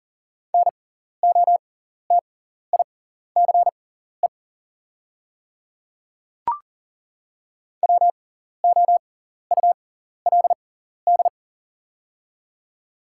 Morse code, heard as a keyed tone, about 40 wpm, spaced as for 14 wpm, 690 Hz. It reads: NOTICE WOULD
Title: 6x Character Spacing - Top 500 Words - 40wpm